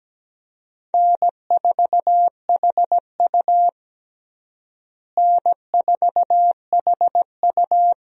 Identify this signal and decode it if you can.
Morse code, heard as a keyed tone, about 17 wpm, 700 Hz. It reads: N4HU N4HU
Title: USA Call Signs - 4 Characters - Repeated 3x - 17wpm